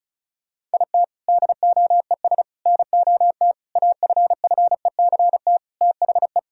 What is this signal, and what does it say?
Morse code, heard as a keyed tone, about 36 wpm, 700 Hz. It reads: IT DOES NOT AFFECT THE
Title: Sentences from Top 1000 Words - Review Part 6 - 35wpm